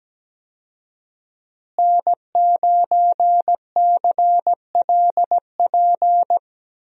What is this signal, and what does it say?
Morse code, heard as a keyed tone, about 17 wpm, 700 Hz. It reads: N9CLP
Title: USA Call Signs - 5 Characters - Repeated 2x - 17wpm